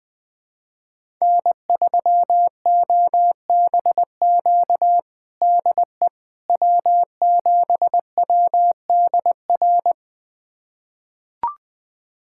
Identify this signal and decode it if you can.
Morse code, heard as a keyed tone, about 20 wpm, 700 Hz. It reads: N3OBQ DE W7WDR
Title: USA Callsign DE Callsign - 5 Characters - 20wpm